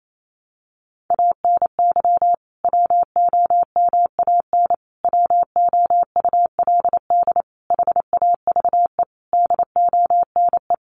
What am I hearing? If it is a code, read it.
Morse code, heard as a keyed tone, about 28 wpm, 700 Hz. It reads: ANY WOMAN WOULD HAVE DONE
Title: Sentences from Top 1100 Words 28wpm